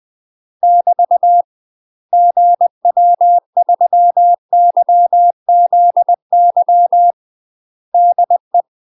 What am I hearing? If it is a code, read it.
Morse code, heard as a keyed tone, about 20 wpm, 700 Hz. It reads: <BT> GW3YZY DE
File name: UK MASTER QSOs TRAINING / 20WPM / UK MASTER QSOs 0045